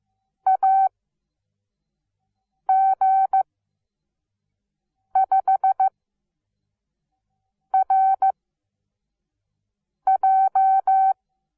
Morse code, heard as a keyed tone, about 15 wpm, spaced as for 4 wpm, 770 Hz. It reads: AG5RJ
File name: Onlinelehrgang / morsen / DJ4UF Morsen17